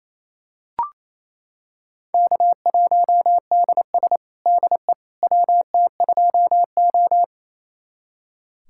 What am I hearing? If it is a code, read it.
Morse code, heard as a keyed tone, about 28 wpm, 700 Hz. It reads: K1DS DE WT2O
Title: USA Callsign DE Callsign - 4 Characters - 28wpm